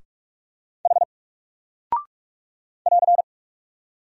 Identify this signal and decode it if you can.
Morse code, heard as a keyed tone, about 44 wpm, 700 Hz. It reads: H <AR>
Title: Single Character (Letter:Number:Punctuation:Prosign) 45wpm